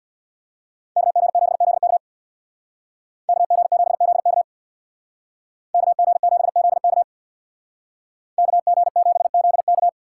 Morse code, heard as a keyed tone, about 64 wpm, 700 Hz. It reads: KG7ZK KG7ZK KG7ZK KG7ZK